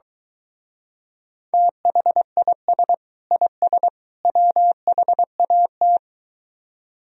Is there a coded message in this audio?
Yes: Morse code from 1.5 to 6.0 s, heard as a keyed tone, about 23 wpm, 700 Hz. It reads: THIS IS WHAT